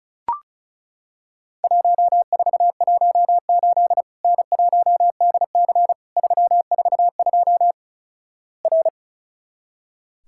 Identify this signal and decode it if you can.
Morse code, heard as a keyed tone, about 36 wpm, 700 Hz. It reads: 1418 N1DC 342 R
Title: Worked All Europe DX Contest - QTC Exchange - 35wpm